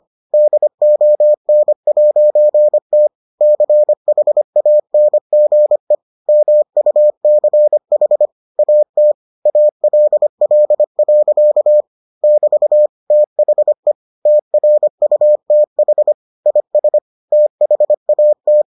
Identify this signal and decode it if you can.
Morse code, heard as a keyed tone, about 25 wpm, 600 Hz. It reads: DON'T CHANGE MUCH AT ALL. <BT> THE TRUTH IS THAT